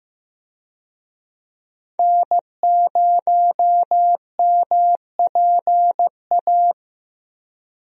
Morse code, heard as a keyed tone, about 15 wpm, 700 Hz. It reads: N0MPA
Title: USA Call Signs - 5 Characters - Repeated 2x - 15wpm